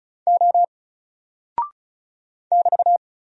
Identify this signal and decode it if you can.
Morse code, heard as a keyed tone, about 36 wpm, 700 Hz. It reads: O <BT>
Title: Single Character (Letter:Number:Punctuation:Prosign) Alt 35wpm